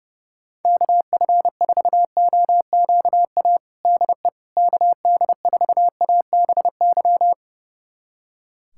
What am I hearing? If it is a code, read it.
Morse code, heard as a keyed tone, about 30 wpm, 700 Hz. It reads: KF4OQA DE KD4ABY
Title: USA Callsign DE Callsign 30wpm